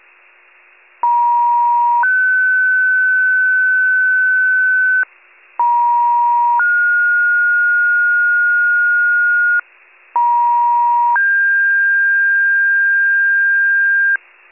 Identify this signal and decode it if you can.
Two-tone page, heard as a one-second tone, then a three-second tone, about 950 then 1500 Hz, about 950 then 1430 Hz, about 950 then 1600 Hz, with noise. It is heard from 1.0 to 14.2 s.